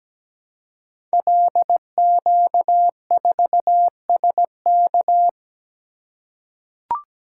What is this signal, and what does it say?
Morse code, heard as a keyed tone, about 17 wpm, 700 Hz. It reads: LQ4SK
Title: Code Groups - 5 Letter:Numbers - 17wpm